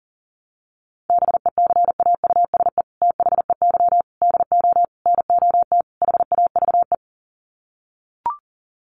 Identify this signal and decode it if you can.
Morse code, heard as a keyed tone, about 37 wpm, 700 Hz. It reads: BECAUSE THEY DO NOT HAVE
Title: Sets of 5 Words Encore2 40wpm